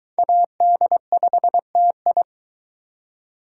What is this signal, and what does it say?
Morse code, heard as a keyed tone, about 23 wpm, 700 Hz. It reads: AD5TI